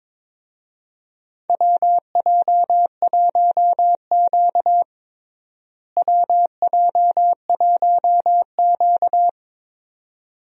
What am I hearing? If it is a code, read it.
Morse code, heard as a keyed tone, about 22 wpm, 700 Hz. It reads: WJ1Q WJ1Q
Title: USA Call Signs - 4 Characters - Repeated 2x - 22wpm